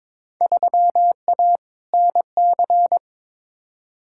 Morse code, heard as a keyed tone, about 22 wpm, 700 Hz. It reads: 3A NC
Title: Field Day Exchanges Encore - 22wpm